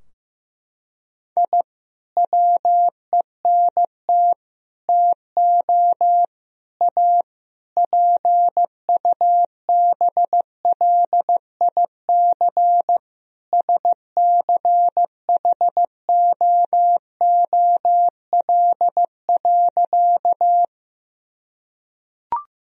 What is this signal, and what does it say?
Morse code, heard as a keyed tone, about 15 wpm, 700 Hz. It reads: I WENT TO A PUBLIC SCHOOL.
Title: Sentences from Top 1000 Words - Review Part 3 - 15wpm